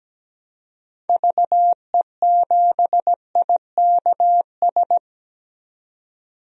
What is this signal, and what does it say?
Morse code, heard as a keyed tone, about 17 wpm, 700 Hz. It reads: VE7IKS
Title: DXCC-100 Call Signs - 6 Characters - 17wpm